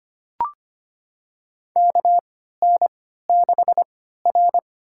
Morse code, heard as a keyed tone, about 25 wpm, spaced as for 14 wpm, 700 Hz. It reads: KN6R
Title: USA Call Signs - 4 Characters - 3x Character Spacing - 25wpm